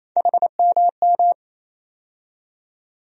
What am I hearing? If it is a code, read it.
Morse code, heard as a keyed tone, about 28 wpm, 700 Hz. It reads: HMM
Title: USA Call Sign Postfixes - 3 Letters 28wpm